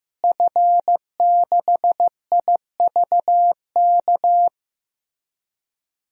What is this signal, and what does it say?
Morse code, heard as a keyed tone, about 15 wpm, 700 Hz. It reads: F6IVK
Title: DXCC-100 Call Signs - 5 Characters - Repeated 3x - 15wpm